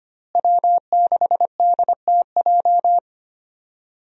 Morse code, heard as a keyed tone, about 25 wpm, 700 Hz. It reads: W6DTJ
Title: USA Call Signs - 5 Characters - Repeated 2x - 25wpm